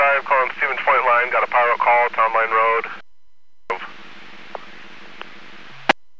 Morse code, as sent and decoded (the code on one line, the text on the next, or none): none